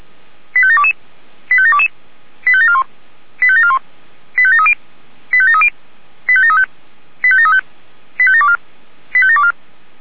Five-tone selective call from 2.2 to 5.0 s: 85741 85741 85730

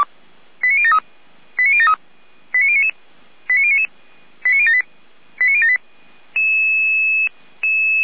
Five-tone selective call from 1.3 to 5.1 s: 89073 89099 89099 89078

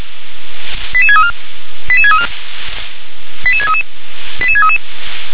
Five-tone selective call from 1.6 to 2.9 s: 88634

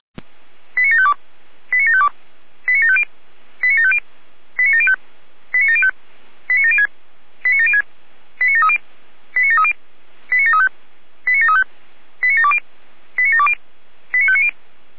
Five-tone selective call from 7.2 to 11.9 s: 89706 89730 89730 89735 89735

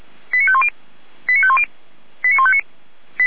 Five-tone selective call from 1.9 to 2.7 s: 89159

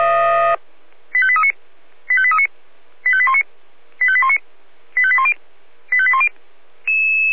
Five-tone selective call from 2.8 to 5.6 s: 85819 85819 85810

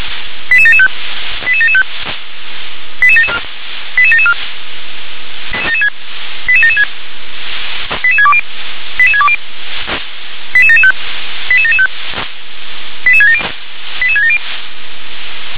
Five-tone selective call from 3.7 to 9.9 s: 88774 88776 88620 88620

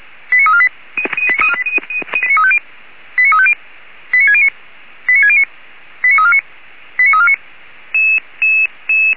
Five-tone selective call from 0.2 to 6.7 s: 89348 89350 89609 89609 89349

